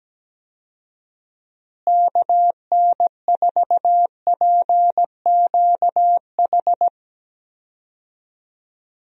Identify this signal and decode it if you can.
Morse code, heard as a keyed tone, about 17 wpm, 700 Hz. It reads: KN4PQH
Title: USA Call Signs - 6 Characters - Repeated 2x - 17wpm